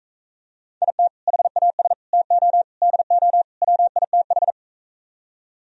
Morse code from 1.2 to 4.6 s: HAS TO DO WITH

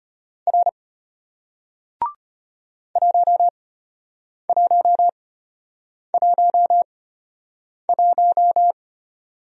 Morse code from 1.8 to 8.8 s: 1111

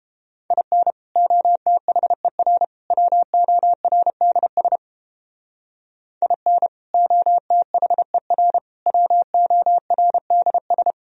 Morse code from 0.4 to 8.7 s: IN OTHER WORDS IN OTHER

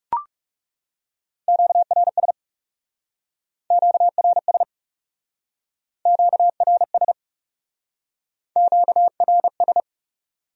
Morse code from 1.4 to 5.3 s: QRS QRS

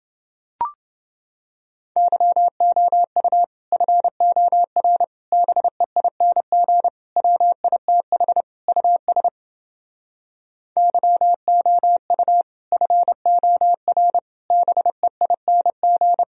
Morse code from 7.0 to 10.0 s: WITH US

